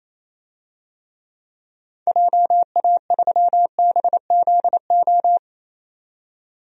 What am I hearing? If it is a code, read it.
Morse code, heard as a keyed tone, about 27 wpm, 700 Hz. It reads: JA3BZO
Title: DXCC-100 Call Signs - 6 Characters - 28wpm